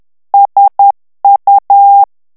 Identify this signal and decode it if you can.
Morse code, heard as a keyed tone, about 10 wpm, 800 Hz. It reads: SU
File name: morse mp3 / ITALIEN texte clair 1 vitesse 10 mots minute